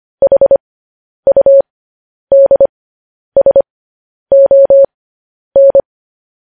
Morse code, heard as a keyed tone, about 25 wpm, spaced as for 10 wpm, 550 Hz. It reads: HUDSON
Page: news headlines in morse code at 10 wpm